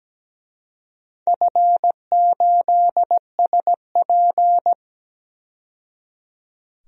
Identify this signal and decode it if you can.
Morse code, heard as a keyed tone, about 17 wpm, 700 Hz. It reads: F8SP
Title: DXCC-100 Call Signs - 4 Characters - 17wpm